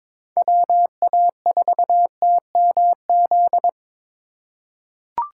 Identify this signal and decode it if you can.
Morse code, heard as a keyed tone, about 22 wpm, 700 Hz. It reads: WA4TMZ
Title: USA Call Signs - 6 Characters - 3x Character Spacing - 22wpm